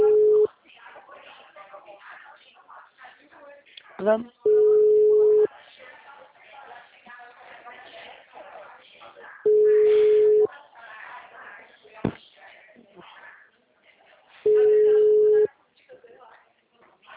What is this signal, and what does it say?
Ringback tone (Europe, 425 Hz), with speech.